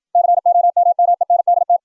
Morse code, heard as a keyed tone, about 60 wpm, 700 Hz. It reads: COMMENDT